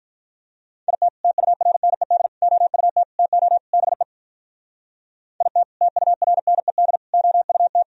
Morse code, heard as a keyed tone, about 51 wpm, 700 Hz. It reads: IT TURNED OUT TO BE IT TURNED OUT